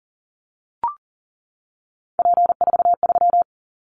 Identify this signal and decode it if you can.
Morse code, heard as a keyed tone, about 40 wpm, 700 Hz. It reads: P43
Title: Callsign Prefixes for Top 100 DXCC - 40wpm